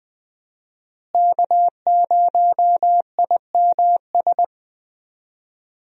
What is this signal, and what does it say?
Morse code, heard as a keyed tone, about 20 wpm, 700 Hz. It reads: K0IMS
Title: USA Call Signs - 5 Characters - Repeated 2x - 20wpm